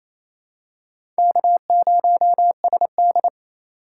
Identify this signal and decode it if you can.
Morse code, heard as a keyed tone, about 28 wpm, 700 Hz. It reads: K0SD
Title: USA Call Signs - 4 Characters - 4x Character Spacing - 28wpm